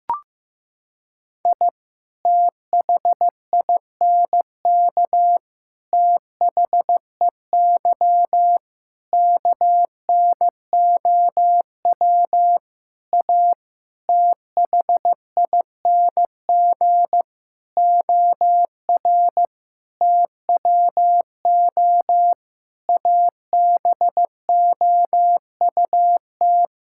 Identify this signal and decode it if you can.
Morse code, heard as a keyed tone, about 15 wpm, 700 Hz. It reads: I THINK THEY KNOW A THING OR TWO ABOUT
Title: Sentences from Top 1000 Words - Review Part 5 - 15wpm